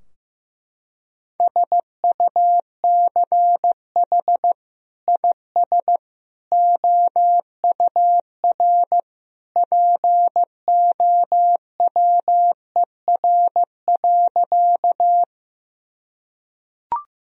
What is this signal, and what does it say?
Morse code, heard as a keyed tone, about 15 wpm, 700 Hz. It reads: SUCH IS OUR POWER.
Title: Sentences from Top 1000 Words - Review Part 2 - 15wpm